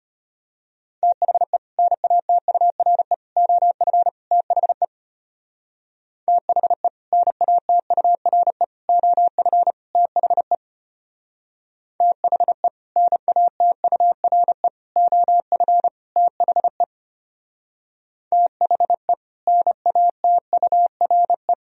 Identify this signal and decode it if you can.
Morse code, heard as a keyed tone, about 32 wpm, 700 Hz. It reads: THE NATURE OF THE THE NATURE OF THE THE NATURE OF THE THE NATURE